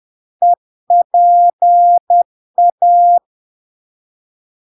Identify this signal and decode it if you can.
Morse code, heard as a keyed tone, about 10 wpm, 700 Hz. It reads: EPA
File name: LARK-FD-2022-Log-in-CW-10wpm-700-Hz